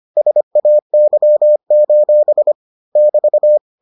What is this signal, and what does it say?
Morse code, heard as a keyed tone, about 25 wpm, 600 Hz. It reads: SAY: <BT>